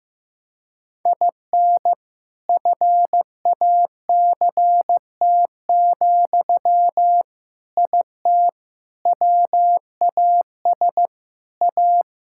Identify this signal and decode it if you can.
Morse code, heard as a keyed tone, about 15 wpm, 700 Hz. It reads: IN FACT, IT WAS A
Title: Sentences from Top 1000 Words - Review Part 6 - 15wpm